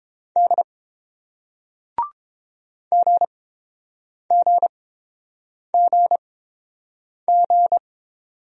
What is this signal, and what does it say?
Morse code, heard as a keyed tone, about 30 wpm, spaced as for 7 wpm, 700 Hz. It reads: D GGGG